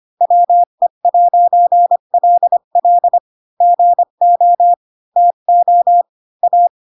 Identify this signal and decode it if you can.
Morse code, heard as a keyed tone, about 25 wpm, 700 Hz. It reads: WE'LL GO TO A
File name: Pinocchio / Pinocchio ch8 20 25 ik1hge